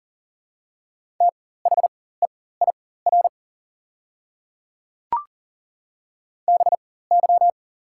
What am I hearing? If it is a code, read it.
Morse code, heard as a keyed tone, about 38 wpm, spaced as for 18 wpm, 690 Hz. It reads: THEIR BY
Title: 4x Character Spacing - Top 100 Words - 40wpm